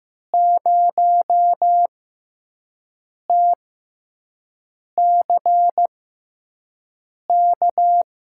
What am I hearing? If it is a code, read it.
Morse code, heard as a keyed tone, about 15 wpm, spaced as for 5 wpm, 700 Hz. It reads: 0TCK
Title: USA Call Signs - 6x Character Spacing - 15wpm